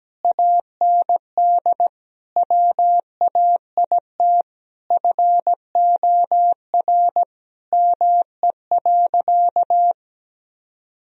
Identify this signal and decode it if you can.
Morse code, heard as a keyed tone, about 17 wpm, 700 Hz. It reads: AND WAIT FOR ME.
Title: Sentences from Top 1000 Words - Review Part 1 - 17wpm